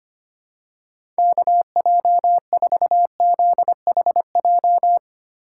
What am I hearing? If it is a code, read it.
Morse code, heard as a keyed tone, about 25 wpm, 700 Hz. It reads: KJ4ZHJ
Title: USA Call Signs - 6 Characters - 2x Character Spacing - 25wpm